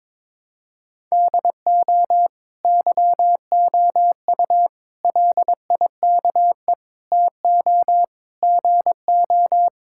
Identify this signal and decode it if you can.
Morse code, heard as a keyed tone, about 22 wpm, 700 Hz. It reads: DO YOU LIKE TO GO
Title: Sentences from Top 1000 Words - Review Part 5 - 22wpm